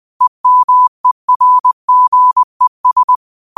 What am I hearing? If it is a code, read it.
Morse code, heard as a keyed tone, about 20 wpm, 1000 Hz. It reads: EMERGES